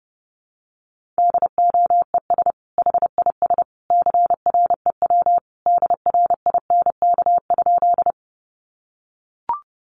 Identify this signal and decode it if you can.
Morse code, heard as a keyed tone, about 30 wpm, 700 Hz. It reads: DOES HIS CREW DRINK?
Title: Sentences from Top 1500 Words 30wpm